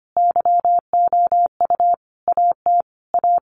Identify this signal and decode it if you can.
Morse code, heard as a keyed tone, about 25 wpm, 700 Hz. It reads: YOU AT A
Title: Sentences from Top 1200 Words 25wpm